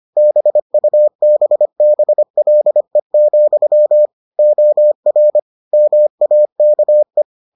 Morse code, heard as a keyed tone, about 25 wpm, 600 Hz. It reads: BUBBLE, OR MAKE